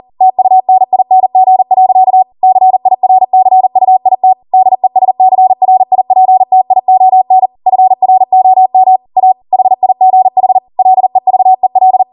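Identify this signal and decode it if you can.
Morse code, heard as a keyed tone, about 40 wpm, 750 Hz. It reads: TUNING. CIRCUIT DESCRIPTION FROM A HIGH LEVEL
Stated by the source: Code practice at 40 wpm.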